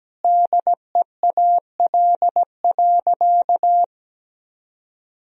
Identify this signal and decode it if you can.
Morse code, heard as a keyed tone, about 17 wpm, 700 Hz. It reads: DEAL.
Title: Sentences from Top 1000 Words - Review Part 4 - 17wpm